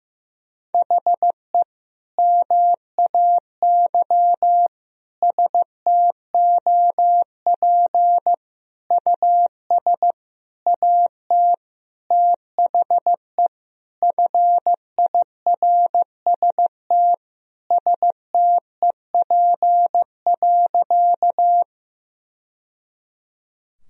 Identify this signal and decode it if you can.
Morse code, heard as a keyed tone, about 15 wpm, 700 Hz. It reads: HE MAY STOP US AT THE FIRST STEP.
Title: Sentences from Top 1000 Words - Review Part 2 - 15wpm